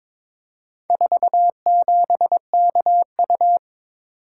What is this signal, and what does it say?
Morse code, heard as a keyed tone, about 23 wpm, 700 Hz. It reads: 47KU
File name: practice / 22wpm / Four Letter-Number Combinations - 22wpm